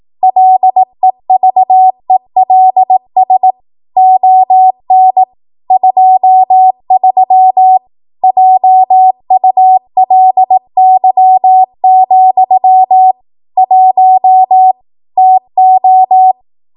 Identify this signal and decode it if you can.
Morse code, heard as a keyed tone, about 18 wpm, 750 Hz. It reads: LEVELS ON 23 JULY, 1 TO
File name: Morse / 18WPM